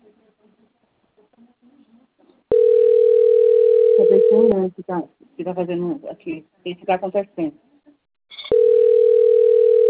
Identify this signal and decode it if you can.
Ringback tone (North America), 440 + 480 Hz, with speech.
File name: rg-1600-unknown- - - .15719